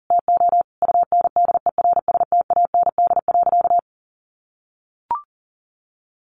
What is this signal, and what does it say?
Morse code, heard as a keyed tone, about 40 wpm, 700 Hz. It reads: TO UNDERSTAND.
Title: Sentences from Top 1100 Words 40wpm